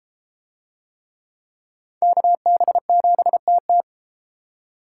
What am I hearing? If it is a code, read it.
Morse code, heard as a keyed tone, about 28 wpm, 700 Hz. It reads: KB7TT